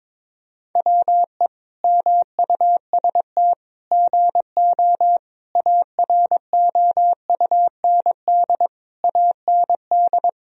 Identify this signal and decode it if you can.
Morse code, heard as a keyed tone, about 22 wpm, 700 Hz. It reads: WE MUST GO AROUND AND
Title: Sentences from Top 1000 Words - Review Part 3 - 22wpm